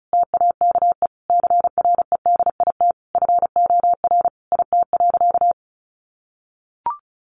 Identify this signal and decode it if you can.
Morse code, heard as a keyed tone, about 34 wpm, 700 Hz. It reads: TAKE CREDIT FOR IT.
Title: Sentences from Top 1000 Words 35wpm